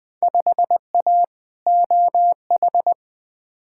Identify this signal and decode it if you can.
Morse code, heard as a keyed tone, about 20 wpm, 700 Hz. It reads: 5A OH
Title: Field Day Exchanges Encore - 20wpm